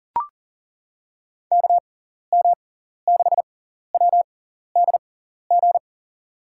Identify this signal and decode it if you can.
Morse code, heard as a keyed tone, about 40 wpm, spaced as for 14 wpm, 700 Hz. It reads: KM6WDG
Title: USA Call Signs - 6 Characters - 6x Character Spacing - 40wpm